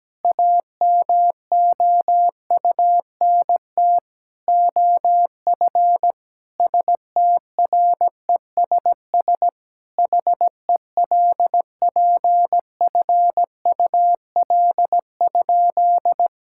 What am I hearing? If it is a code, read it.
Morse code, heard as a keyed tone, about 17 wpm, 700 Hz. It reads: AMOUNT OF STRESS HELPFUL?
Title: Sentences from Top 1000 Words - Review Part 6 - 17wpm